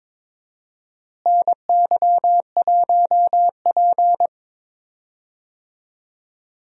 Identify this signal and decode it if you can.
Morse code, heard as a keyed tone, about 21 wpm, 700 Hz. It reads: NY1P
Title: USA Call Signs - 4 Characters - Repeated 2x - 22wpm